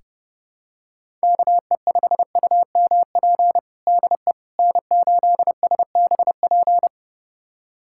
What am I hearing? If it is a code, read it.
Morse code, heard as a keyed tone, about 30 wpm, 700 Hz. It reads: KE5UMP DE N8SBP
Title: USA Callsign DE Callsign 30wpm